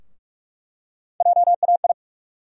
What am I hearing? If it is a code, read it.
Morse code, heard as a keyed tone, about 46 wpm, 700 Hz. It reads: JAI
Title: USA Call Sign Postfixes - 3 Letters 45wpm